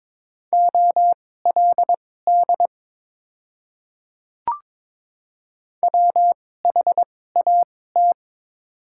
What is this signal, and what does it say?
Morse code, heard as a keyed tone, about 23 wpm, spaced as for 16 wpm, 700 Hz. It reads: OLD WHAT
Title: 2x Character Spacing - Top 200 Words - 22wpm